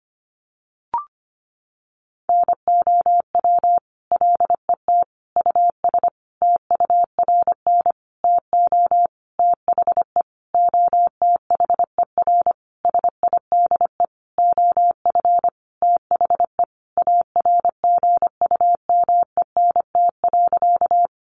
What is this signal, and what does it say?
Morse code, heard as a keyed tone, about 25 wpm, 700 Hz. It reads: NOW LET US TURN TO THE OTHER SIDE OF THE ARGUMENT.
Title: Sentences from Top 1400 Words 25wpm